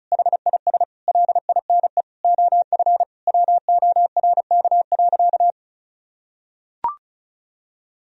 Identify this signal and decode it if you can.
Morse code, heard as a keyed tone, about 34 wpm, 700 Hz. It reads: HIS LINE OF WORK.
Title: Sentences from Top 1000 Words - Review Part 4 - 35wpm